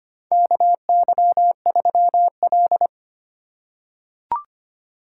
Morse code, heard as a keyed tone, about 25 wpm, 700 Hz. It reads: KY3L
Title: USA Call Signs - 4 Characters - 2x Character Spacing - 25wpm